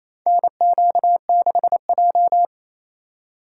Morse code, heard as a keyed tone, about 28 wpm, 700 Hz. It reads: NQ6J
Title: USA Call Signs - 8x Character Spacing - 28wpm